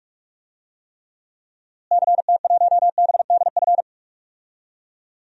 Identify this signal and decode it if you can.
Morse code, heard as a keyed tone, about 46 wpm, 700 Hz. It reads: CT1BDF
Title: DXCC-100 Call Signs - 6 Characters - Repeated 3x - 45wpm